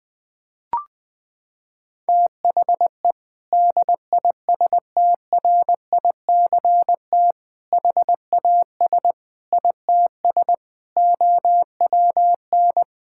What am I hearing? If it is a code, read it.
Morse code, heard as a keyed tone, about 20 wpm, 700 Hz. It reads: THE DISTRICT HAS ITS OWN
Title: Sentences from Top 1600 Words 20wpm